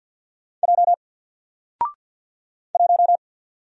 Morse code, heard as a keyed tone, about 50 wpm, 700 Hz. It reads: J 1
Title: Single Letter-Number 50wpm